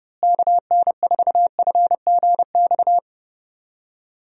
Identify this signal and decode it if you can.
Morse code, heard as a keyed tone, about 30 wpm, 700 Hz. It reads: KN4FGX
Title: USA Call Signs - 6 Characters - Repeated 2x - 30wpm